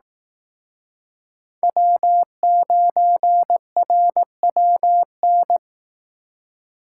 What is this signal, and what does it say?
Morse code, heard as a keyed tone, about 19 wpm, 700 Hz. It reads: W9RWN
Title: Speed Racing - USA 1x3 Call Signs 15wpm